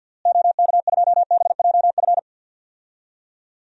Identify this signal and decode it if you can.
Morse code, heard as a keyed tone, about 51 wpm, 700 Hz. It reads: OK2BJF
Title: DXCC-100 Call Signs - 6 Characters - Repeated 3x - 50wpm